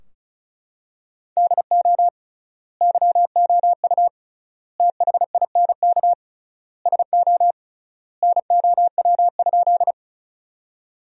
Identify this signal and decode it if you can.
Morse code, heard as a keyed tone, about 36 wpm, 700 Hz. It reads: DO YOU THINK SO NOW?
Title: 3x Word-Spacing - Sentences from Top 200 Words - 35wpm